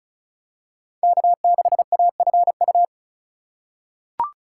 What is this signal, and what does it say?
Morse code, heard as a keyed tone, about 36 wpm, 700 Hz. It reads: K6AFU
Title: USA Call Signs - 5 Characters - 4x Character Spacing - 35wpm